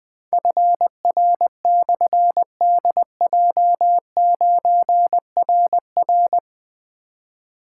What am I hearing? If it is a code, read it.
Morse code, heard as a keyed tone, about 20 wpm, 700 Hz. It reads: FR/DJ9RR
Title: Difficult Call Signs 20wpm